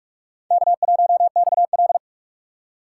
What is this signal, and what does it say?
Morse code, heard as a keyed tone, about 44 wpm, 700 Hz. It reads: K1XL
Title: USA Call Signs - 4 Characters - 2x Character Spacing - 45wpm